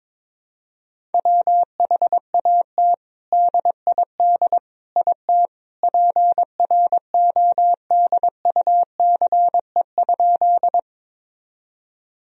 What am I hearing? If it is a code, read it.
Morse code, heard as a keyed tone, about 21 wpm, 700 Hz. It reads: WHAT DID IT PRODUCE?
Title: Sentences from Top 1000 Words - Review Part 3 - 22wpm